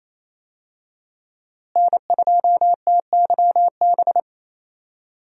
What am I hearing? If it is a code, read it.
Morse code, heard as a keyed tone, about 28 wpm, 700 Hz. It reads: N2TYB
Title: USA Call Signs - 5 Characters - Repeated 3x - 28wpm